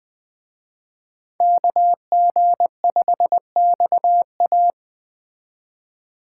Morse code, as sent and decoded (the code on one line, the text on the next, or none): -.- --. ..... -..- .-
KG5XA